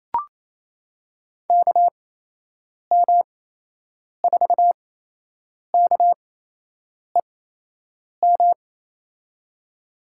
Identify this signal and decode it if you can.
Morse code, heard as a keyed tone, about 28 wpm, spaced as for 8 wpm, 700 Hz. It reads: KM4KEM
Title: USA Call Signs - 8x Character Spacing - 28wpm